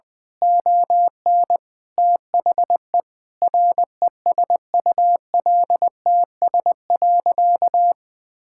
Morse code, heard as a keyed tone, about 21 wpm, 700 Hz. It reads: ON THE RESULTS.
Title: Sentences with 7-Word Phrases - Spoken Beginning - 20wpm